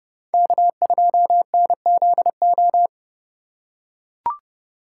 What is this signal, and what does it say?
Morse code, heard as a keyed tone, about 29 wpm, 700 Hz. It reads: K2NZO